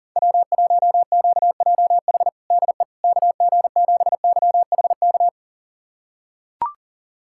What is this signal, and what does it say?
Morse code, heard as a keyed tone, about 40 wpm, 700 Hz. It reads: W1QJH DE KG7YHK
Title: USA Callsign DE Callsign 40wpm